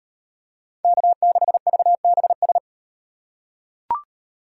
Morse code, heard as a keyed tone, about 37 wpm, 700 Hz. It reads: K6VBS